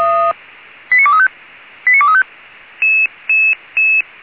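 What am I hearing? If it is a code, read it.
Five-tone selective call (ZVEI), heard as quick runs of five tones, tones of about 80 ms. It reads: 89235 89235
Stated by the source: Five-tone ZVEI selective call received on an SDR.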